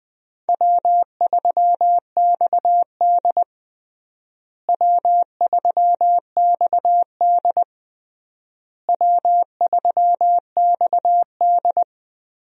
Morse code, heard as a keyed tone, about 20 wpm, 700 Hz. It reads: W3XD W3XD W3XD
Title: USA Call Signs - 4 Characters - Repeated 3x - 20wpm